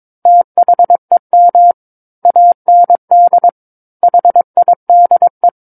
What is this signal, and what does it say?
Morse code, heard as a keyed tone, about 22 wpm, 700 Hz. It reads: THEM AND HIDE
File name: Book One / 13-Chapter 13 - 22 WPM